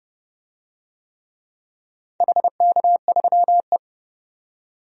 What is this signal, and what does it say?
Morse code, heard as a keyed tone, about 30 wpm, 700 Hz. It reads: HK3E
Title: DXCC-100 Call Signs - 4 Characters - 30wpm